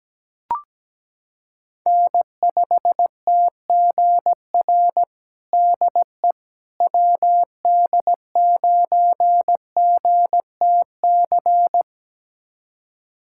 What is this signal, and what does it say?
Morse code, heard as a keyed tone, about 17 wpm, 700 Hz. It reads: N5TGR DE WD9GTC
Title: USA Callsign DE Callsign 17wpm